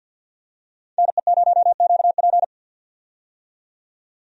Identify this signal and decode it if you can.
Morse code, heard as a keyed tone, about 52 wpm, 700 Hz. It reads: NE0QP
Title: USA Call Signs - 5 Characters - 50wpm